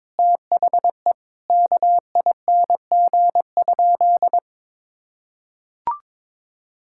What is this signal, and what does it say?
Morse code, heard as a keyed tone, about 22 wpm, 700 Hz. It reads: THE KING?
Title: Sentences from Top 1600 Words 22wpm